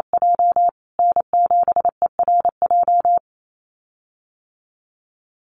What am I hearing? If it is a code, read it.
Morse code, heard as a keyed tone, about 28 wpm, 700 Hz. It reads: J N7ERJ
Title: USA Call Signs Repeated 2x 28wpm